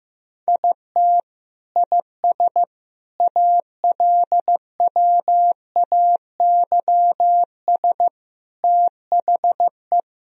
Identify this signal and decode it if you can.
Morse code, heard as a keyed tone, about 16 wpm, 700 Hz. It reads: IT IS ALWAYS THE
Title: Sentences from Top 1000 Words - Review Part 1 - 15wpm